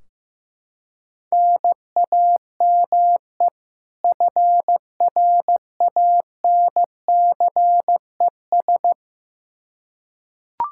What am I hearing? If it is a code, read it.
Morse code, heard as a keyed tone, about 15 wpm, 700 Hz. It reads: NAME FRANCES
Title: Practice QSO - Exchange 1 - Beginning - 15wpm